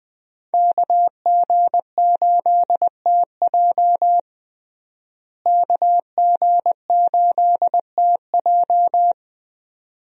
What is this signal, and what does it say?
Morse code, heard as a keyed tone, about 20 wpm, 700 Hz. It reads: KG8TJ KG8TJ
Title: USA Call Signs - 5 Characters - Repeated 2x - 20wpm